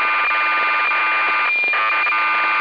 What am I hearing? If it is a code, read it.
Cassette data (Kansas City type).